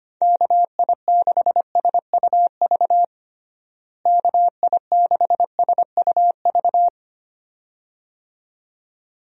Morse code, heard as a keyed tone, about 25 wpm, 700 Hz. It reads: KI6SUV KI6SUV
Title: USA Call Signs - 6 Characters - Repeated 2x - 25wpm